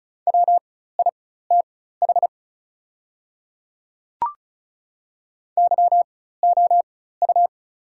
Morse code, heard as a keyed tone, about 36 wpm, spaced as for 16 wpm, 700 Hz. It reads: WITH YOU
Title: 4x Character Spacing - Top 200 Words - 35wpm